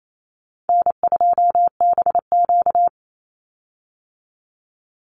Morse code, heard as a keyed tone, about 28 wpm, 700 Hz. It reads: N2BQ
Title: USA Call Signs 28wpm